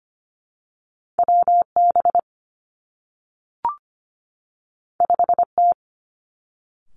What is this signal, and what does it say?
Morse code, heard as a keyed tone, about 25 wpm, 700 Hz. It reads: W B 5 T
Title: Two Challenging Letter-Number Combinations 25wpm